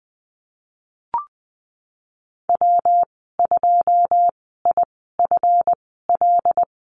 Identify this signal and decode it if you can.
Morse code, heard as a keyed tone, about 20 wpm, spaced as for 15 wpm, 700 Hz. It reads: W2IFL
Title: 154 Lesson 25g - Letter p - Call Signs 20wpm